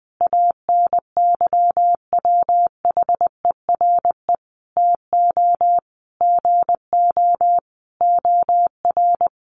Morse code, heard as a keyed tone, about 20 wpm, 700 Hz. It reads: ANYWHERE TO GO OR
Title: Sentences from Top 1400 Words 20wpm